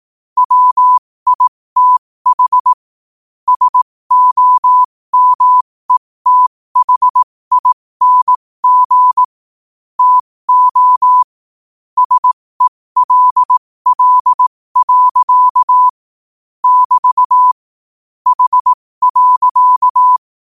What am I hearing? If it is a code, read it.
Morse code, heard as a keyed tone, about 18 wpm, spaced as for 15 wpm, 1000 Hz. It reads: WITH SOMETHING TO SELL. <BT> H.
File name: qotdfiles / 15wpm